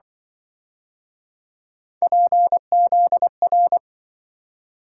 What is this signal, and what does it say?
Morse code, heard as a keyed tone, about 26 wpm, 700 Hz. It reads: PZR